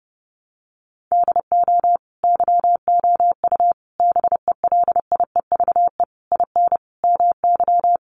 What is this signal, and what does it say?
Morse code, heard as a keyed tone, about 30 wpm, 700 Hz. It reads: DO YOU BELIEVE IN MY
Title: Sentences from Top 1200 Words 30wpm